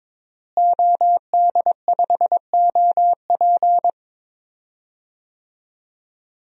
Morse code, heard as a keyed tone, about 22 wpm, 700 Hz. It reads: OD5OP
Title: DXCC-100 Call Signs - 5 Characters - Repeated 2x - 22wpm